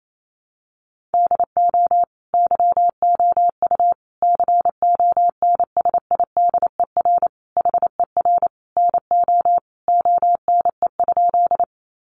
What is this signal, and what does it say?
Morse code, heard as a keyed tone, about 28 wpm, 700 Hz. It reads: DO YOU CONSIDER HER NO ONE?
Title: Sentences from Top 600 Words 28wpm